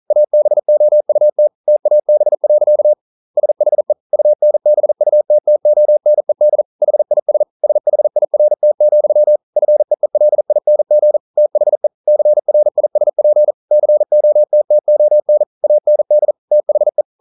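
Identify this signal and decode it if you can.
Morse code, heard as a keyed tone, about 38 wpm, 600 Hz. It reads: ABOUT TAB. SHE UNBUTTONED HIS SHIRT, FEELING THE CRISP COTTON AND THE
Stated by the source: Morse code practice at 41 wpm, from CW med Gnister.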